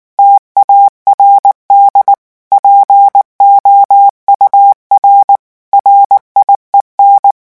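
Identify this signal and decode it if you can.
Morse code, heard as a keyed tone, about 19 wpm, 800 Hz. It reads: TARD POUR RIEN
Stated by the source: French Morse revision 8 drill of random five-letter groups at 18 words per minute.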